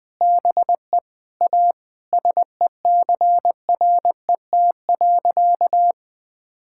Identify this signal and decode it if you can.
Morse code, heard as a keyed tone, about 20 wpm, 700 Hz. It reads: BE A SECRET.
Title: Sentences with 4-Word Phrases - 20wpm